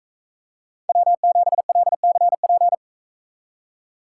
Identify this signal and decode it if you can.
Morse code, heard as a keyed tone, about 44 wpm, 700 Hz. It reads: W7LCP